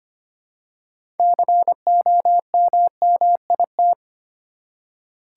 Morse code, heard as a keyed tone, about 25 wpm, 700 Hz. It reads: COMMIT
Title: Six Letter Words - 25wpm